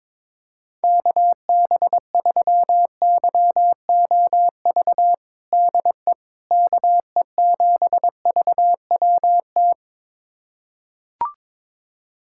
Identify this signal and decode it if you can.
Morse code, heard as a keyed tone, about 22 wpm, 700 Hz. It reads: KB3YOV DE KE7VWT
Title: USA Callsign DE Callsign - 6 Characters - 22wpm